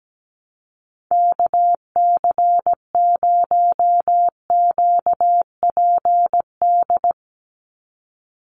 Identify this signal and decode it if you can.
Morse code, heard as a keyed tone, about 17 wpm, 700 Hz. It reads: KC0QPD
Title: USA Call Signs 17wpm